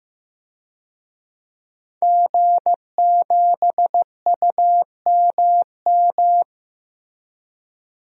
Morse code, heard as a keyed tone, about 15 wpm, 700 Hz. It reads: G7UMM
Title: DXCC-100 Call Signs - 5 Characters - Repeated 2x - 15wpm